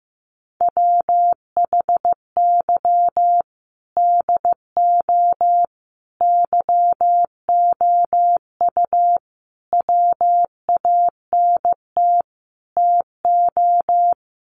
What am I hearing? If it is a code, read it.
Morse code, heard as a keyed tone, about 15 wpm, 700 Hz. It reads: WHY DO YOU WANT TO
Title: Sets of 5 Words Encore2 15wpm